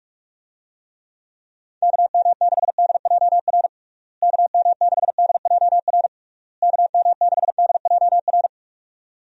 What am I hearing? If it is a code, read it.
Morse code, heard as a keyed tone, about 48 wpm, 700 Hz. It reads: KM6DJR KM6DJR KM6DJR
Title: USA Call Signs - 6 Characters - Repeated 3x - 45wpm